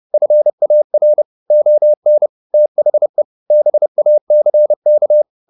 Morse code, heard as a keyed tone, about 30 wpm, 600 Hz. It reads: FAR ON THE BACK